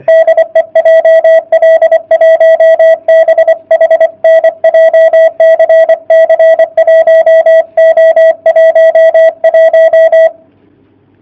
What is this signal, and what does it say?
Morse code, heard as a keyed tone, about 24 wpm, 660 Hz. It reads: DE JL1BHN JCC 1O11
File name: jh8cbh / 2019cw / sp31- w